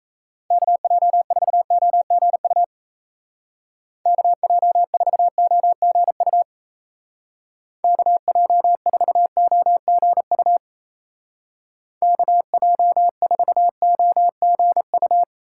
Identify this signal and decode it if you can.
Morse code, heard as a keyed tone, about 34 wpm, 700 Hz. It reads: KJ4OGU KJ4OGU KJ4OGU KJ4OGU
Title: Speed Racing - USA 2x3 Call Signs 28wpm